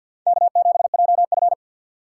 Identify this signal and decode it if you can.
Morse code, heard as a keyed tone, about 51 wpm, 690 Hz. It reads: K7JF
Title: USA Call Signs - 2x Character Spacing - 50wpm